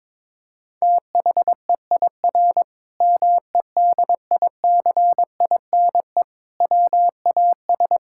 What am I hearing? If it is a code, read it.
Morse code, heard as a keyed tone, about 22 wpm, 700 Hz. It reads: THEIR MEDICINE WAS
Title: Sentences from Top 1000 Words - Review Part 6 - 22wpm